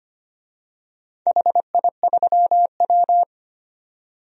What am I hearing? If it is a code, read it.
Morse code, heard as a keyed tone, about 25 wpm, 700 Hz. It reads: HI3W
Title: DXCC-100 Call Signs - 4 Characters - 25wpm